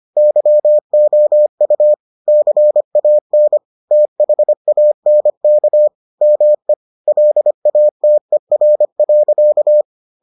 Morse code, heard as a keyed tone, about 25 wpm, 600 Hz. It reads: YOU CAN THANK ME LATER.